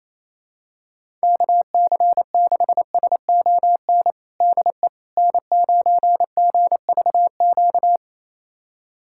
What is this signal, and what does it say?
Morse code, heard as a keyed tone, about 28 wpm, 700 Hz. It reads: KC6SON DE N9GVQ
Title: USA Callsign DE Callsign 28wpm